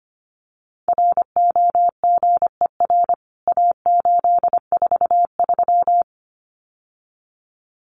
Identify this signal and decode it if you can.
Morse code, heard as a keyed tone, about 25 wpm, 700 Hz. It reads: ROGER A843
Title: CWT Exchange 25wpm